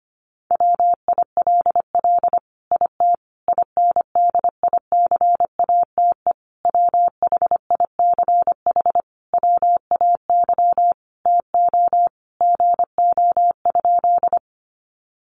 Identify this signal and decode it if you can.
Morse code, heard as a keyed tone, about 25 wpm, 700 Hz. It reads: WILL IT INDICATE WHICH WAY TO GO?
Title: Sentences from Top 1300 Words 25wpm